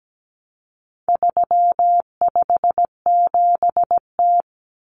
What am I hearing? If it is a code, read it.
Morse code, heard as a keyed tone, about 17 wpm, 700 Hz. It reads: 357T
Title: Common Cut Numbers - 4-Digit 17wpm